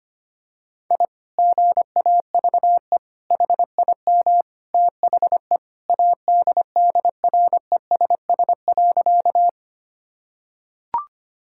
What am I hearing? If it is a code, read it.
Morse code, heard as a keyed tone, about 25 wpm, 700 Hz. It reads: I GAVE HIM THE ADDRESS.
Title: Sentences from Top 1000 Words - Review Part 6 - 25wpm